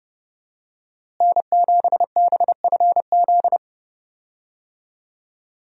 Morse code, heard as a keyed tone, about 30 wpm, 700 Hz. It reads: N7BFZ
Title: USA Call Signs - 5 Characters - Repeated 2x - 30wpm